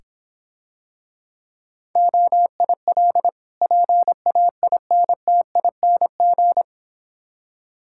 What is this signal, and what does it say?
Morse code, heard as a keyed tone, about 26 wpm, 700 Hz. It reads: OIL PAINTING